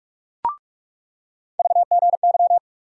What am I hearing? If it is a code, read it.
Morse code, heard as a keyed tone, about 46 wpm, 700 Hz. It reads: VGY
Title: USA Call Sign Postfixes - 3 Letters 45wpm